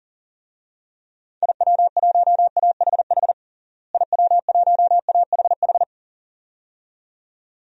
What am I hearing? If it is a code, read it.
Morse code, heard as a keyed tone, about 37 wpm, 700 Hz. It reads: IW1AHH IW1AHH
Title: DXCC-100 Call Signs - 6 Characters - Repeated 2x - 40wpm